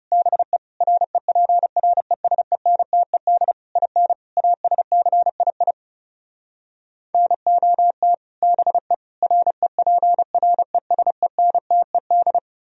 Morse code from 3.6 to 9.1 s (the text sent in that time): IN ASCII NOT BE